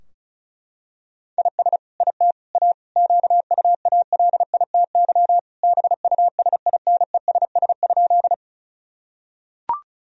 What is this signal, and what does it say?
Morse code, heard as a keyed tone, about 36 wpm, 700 Hz. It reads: IS IT A QUALITY BUSINESS?
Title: Sentences from Top 1000 Words - Review Part 4 - 35wpm